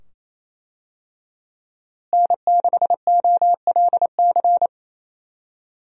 Morse code, heard as a keyed tone, about 28 wpm, 700 Hz. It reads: N6OLC
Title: USA Call Signs - 5 Characters - Repeated 2x - 28wpm